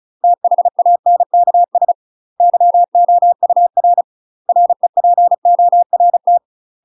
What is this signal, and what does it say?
Morse code, heard as a keyed tone, about 36 wpm, 700 Hz. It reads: THANKS YOUR REPORT
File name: UK MASTER QSOs TRAINING / 35WPM / UK MASTER QSOs 0020